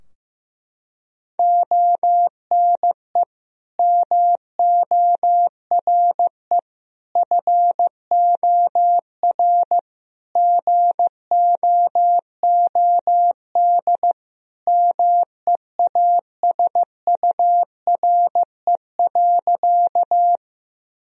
Morse code, heard as a keyed tone, about 16 wpm, 700 Hz. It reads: ONE MORE FOR GOOD MEASURE.
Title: Sentences from Top 1000 Words - Review Part 2 - 15wpm